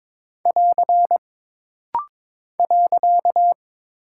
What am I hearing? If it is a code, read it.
Morse code, heard as a keyed tone, about 22 wpm, 700 Hz. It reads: <AR> .
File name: practice / 22wpm / Single Punctuation-Prosign - Rapid-Fire - 22wpm